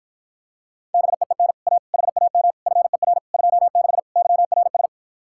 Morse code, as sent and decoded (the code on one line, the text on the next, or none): -... . -. / .- / .... .- -- / ..-. .-. / ..--- -.... / -.-- .-. ...
BEEN A HAM FER 26 YRS